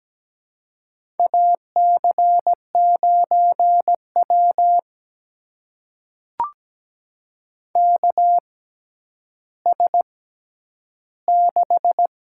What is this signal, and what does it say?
Morse code, heard as a keyed tone, about 17 wpm, spaced as for 10 wpm, 700 Hz. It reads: AC9W KS6
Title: USA Call Signs - 4 Characters - 6x Character Spacing - 17wpm